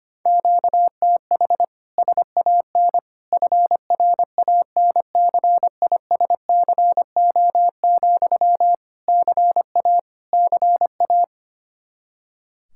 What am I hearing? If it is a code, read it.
Morse code, heard as a keyed tone, about 25 wpm, 700 Hz. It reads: QTH SAN FRANCISCO, CA CA
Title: K1USN SST QSO Practice 25wpm